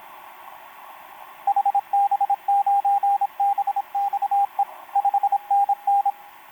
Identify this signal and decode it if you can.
Morse code, heard as a keyed tone, about 27 wpm, 800 Hz, with radio noise. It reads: HB9BXE 5NN
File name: pj4-w9njy f5upo etc 12m cw 28mar2011 1924